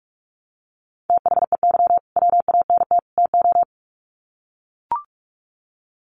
Morse code, heard as a keyed tone, about 44 wpm, 700 Hz. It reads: THEY WANT TO
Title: Sets of 3 Words 45wpm